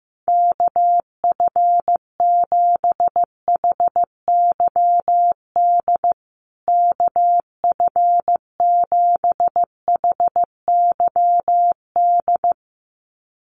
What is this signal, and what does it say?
Morse code, heard as a keyed tone, about 14 wpm, 700 Hz. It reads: KF7HYD KF7HYD
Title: USA Call Signs Repeated 2x 15wpm